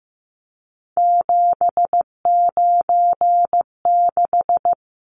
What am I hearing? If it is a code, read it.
Morse code, heard as a keyed tone, about 15 wpm, 700 Hz. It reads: 796
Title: Numbers - 3-Digit 15wpm